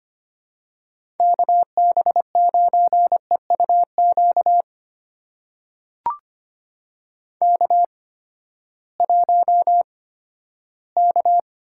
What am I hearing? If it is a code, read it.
Morse code, heard as a keyed tone, about 25 wpm, spaced as for 14 wpm, 700 Hz. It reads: KB9EUQ K1K
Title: USA Call Signs - 8x Character Spacing - 25wpm